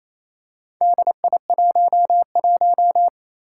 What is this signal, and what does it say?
Morse code, heard as a keyed tone, about 28 wpm, 700 Hz. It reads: DI11
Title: Four Challenging Letter-Number Combinations 28wpm